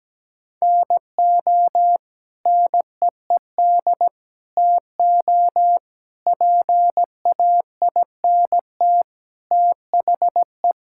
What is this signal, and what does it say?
Morse code, heard as a keyed tone, about 17 wpm, 700 Hz. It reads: NO NEED TO PAINT THE
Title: Sentences from Top 1000 Words - Review Part 6 - 17wpm